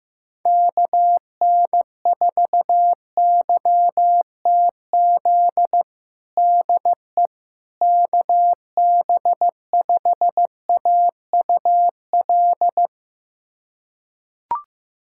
Morse code, heard as a keyed tone, about 15 wpm, 700 Hz. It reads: KN4YTZ DE KB5AUL
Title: USA Callsign DE Callsign - 6 Characters - 15wpm